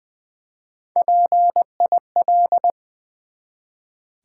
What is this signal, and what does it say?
Morse code, heard as a keyed tone, about 20 wpm, 700 Hz. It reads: PIL
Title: Build A Word - 1 Letter At A Time - 4-Letter Words - 20wpm